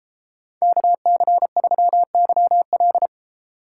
Morse code, heard as a keyed tone, about 32 wpm, 700 Hz. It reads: KC3YL